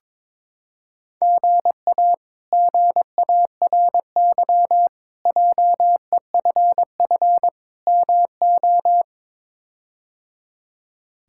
Morse code, heard as a keyed tone, about 22 wpm, 700 Hz. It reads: GA GARY JEFF MO
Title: K1USN SST QSO Practice - 22wpm